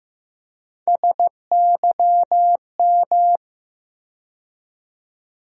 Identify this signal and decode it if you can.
Morse code, heard as a keyed tone, about 15 wpm, 700 Hz. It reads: SYM
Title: USA Call Sign Postfixes - 3 Letters 15wpm